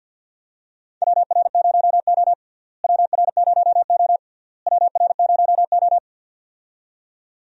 Morse code, heard as a keyed tone, about 46 wpm, 700 Hz. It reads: WR0O WR0O WR0O
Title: USA Call Signs - 4 Characters - Repeated 3x - 50wpm